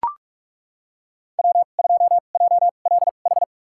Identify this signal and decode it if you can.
Morse code, heard as a keyed tone, about 42 wpm, spaced as for 31 wpm, 700 Hz. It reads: W2JLH
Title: USA Call Signs - 5 Characters - 2x Character Spacing - 45wpm